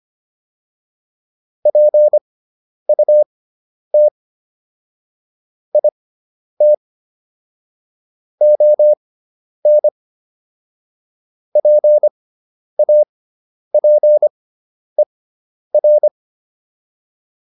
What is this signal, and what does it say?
Morse code, heard as a keyed tone, about 25 wpm, spaced as for 10 wpm, 600 Hz. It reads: PUT IT ON PAPER